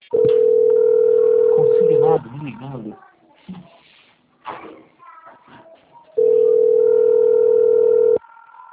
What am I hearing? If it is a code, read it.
Ringback tone (North America), 440 + 480 Hz, with speech.
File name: out- - - - - .3817